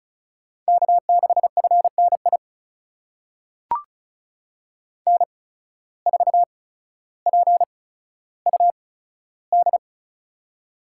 Morse code, heard as a keyed tone, about 34 wpm, spaced as for 15 wpm, 700 Hz. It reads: K6FNI N4PUD
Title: USA Call Signs - 5 Characters - 8x Character Spacing - 35wpm